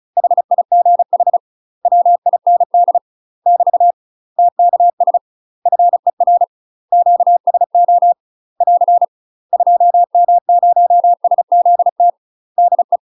Morse code, heard as a keyed tone, about 34 wpm, 700 Hz. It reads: HIGH WIND <BT> TKS FER QSO <AR> 2M0SZT DE
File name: UK MASTER QSOs TRAINING / 35WPM / UK MASTER QSOs 0025